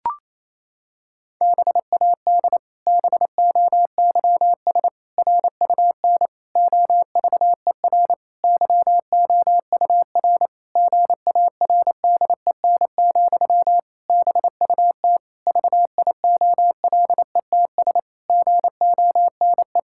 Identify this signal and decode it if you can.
Morse code, heard as a keyed tone, about 28 wpm, 700 Hz. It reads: BAD BOYS RUN OVER YOUR GARDEN, BUT VIOLETS GONE
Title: Pangram Challenge v2 - 28wpm